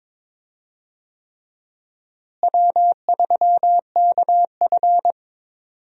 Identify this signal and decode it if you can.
Morse code, heard as a keyed tone, about 22 wpm, 700 Hz. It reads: W3KF
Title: USA Call Signs - 4 Characters - Repeated 3x - 22wpm